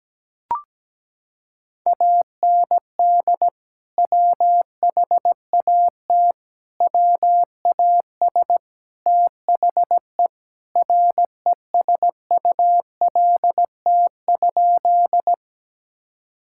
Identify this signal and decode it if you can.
Morse code, heard as a keyed tone, about 17 wpm, 700 Hz. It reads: AND WHAT WAS THE RESULT?
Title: Sentences from Top 1000 Words - Review Part 5 - 17wpm